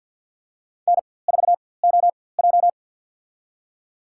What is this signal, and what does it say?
Morse code, heard as a keyed tone, about 53 wpm, spaced as for 24 wpm, 690 Hz. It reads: N4OJ
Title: USA Call Signs - 4x Character Spacing - 50wpm